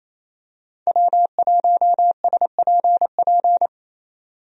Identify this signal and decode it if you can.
Morse code, heard as a keyed tone, about 28 wpm, 700 Hz. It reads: W1SPP
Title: USA Call Signs - 5 Characters - 28wpm